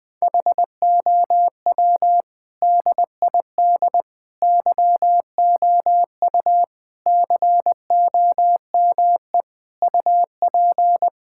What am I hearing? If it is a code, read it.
Morse code, heard as a keyed tone, about 20 wpm, 700 Hz. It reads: HOW DID YOU COME UP
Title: Sentences with 8-Word Phrases - Spoken Beginning - 20wpm